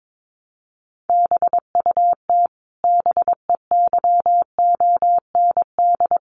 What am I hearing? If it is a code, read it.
Morse code, heard as a keyed tone, about 22 wpm, 700 Hz. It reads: BUT BEYOND
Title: Sentences from Top 1300 Words 22wpm